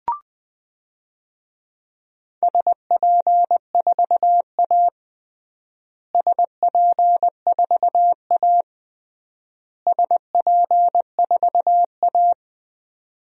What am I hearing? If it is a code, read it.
Morse code, heard as a keyed tone, about 20 wpm, 700 Hz. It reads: SP4A SP4A SP4A
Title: DXCC-100 Call Signs - 4 Characters - Repeated 3x - 20wpm